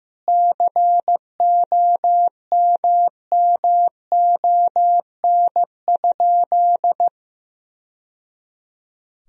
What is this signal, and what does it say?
Morse code, heard as a keyed tone, about 15 wpm, 700 Hz. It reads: COMMON?
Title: Sentences from Top 1000 Words - Review Part 3 - 15wpm